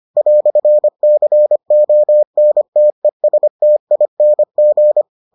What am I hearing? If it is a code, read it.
Morse code, heard as a keyed tone, about 25 wpm, 600 Hz. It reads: "CONTESTING